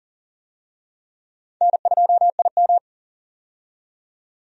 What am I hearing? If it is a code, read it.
Morse code, heard as a keyed tone, about 40 wpm, 700 Hz. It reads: N2IM
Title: USA Call Signs - 4 Characters - 40wpm